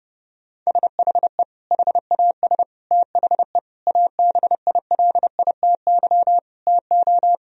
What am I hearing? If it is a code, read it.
Morse code, heard as a keyed tone, about 30 wpm, 700 Hz. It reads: SHE HAS THE ABILITY TO
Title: Sentences from Top 1000 Words - Review Part 3 - 30wpm